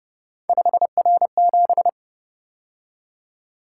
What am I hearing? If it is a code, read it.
Morse code, heard as a keyed tone, about 30 wpm, 700 Hz. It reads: 5R7
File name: Speed-Racing - Three Letter-Number Combinations - 25wpm